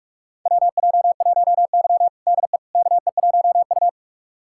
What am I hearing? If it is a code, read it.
Morse code, heard as a keyed tone, about 46 wpm, 700 Hz. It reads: WJ1Y DE KE1U
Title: USA Callsign DE Callsign - 4 Characters - 45wpm